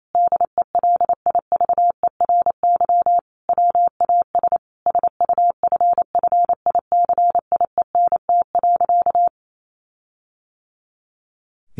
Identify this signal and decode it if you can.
Morse code, heard as a keyed tone, about 28 wpm, 700 Hz. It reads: DELIVERY WAS SUFFICIENT.
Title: Sentences from Top 1500 Words 28wpm